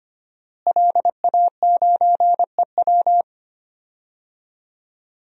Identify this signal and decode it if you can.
Morse code, heard as a keyed tone, about 25 wpm, 700 Hz. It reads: LA9EW
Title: DXCC-100 Call Signs - 5 Characters - Repeated 2x - 25wpm